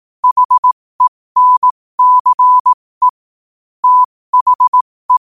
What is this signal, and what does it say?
Morse code, heard as a keyed tone, about 18 wpm, spaced as for 15 wpm, 1000 Hz. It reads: HENCE THE
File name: qotdfiles / 15wpm